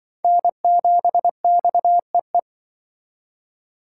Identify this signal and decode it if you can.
Morse code, heard as a keyed tone, about 24 wpm, 700 Hz. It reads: N7XEE
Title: Speed Racing - USA 1x3 Call Signs 20wpm